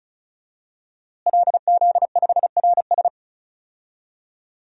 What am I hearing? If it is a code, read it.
Morse code, heard as a keyed tone, about 34 wpm, 700 Hz. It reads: LZ5RS
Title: DXCC-100 Call Signs - 5 Characters - Repeated 2x - 35wpm